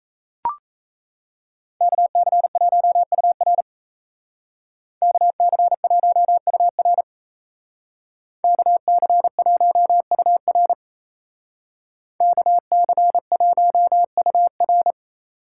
Morse code, heard as a keyed tone, about 36 wpm, 700 Hz. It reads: KC1UR KC1UR KC1UR KC1UR